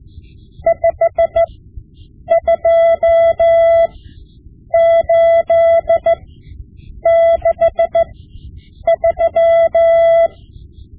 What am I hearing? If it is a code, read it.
Morse code, heard as a keyed tone, about 15 wpm, spaced as for 7 wpm, 670 Hz. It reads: 52863
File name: Morse Lessons / Morse Lessons / Tape-5B